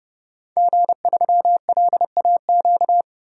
Morse code, heard as a keyed tone, about 30 wpm, 700 Hz. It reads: G3LAQ
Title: DXCC-100 Call Signs - 5 Characters - 30wpm